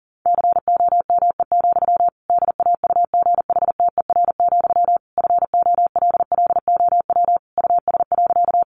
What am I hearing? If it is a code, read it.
Morse code, heard as a keyed tone, about 40 wpm, 700 Hz. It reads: COME, DAUGHTER, FOLLOW US.
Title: Sentences from Top 1500 Words 40wpm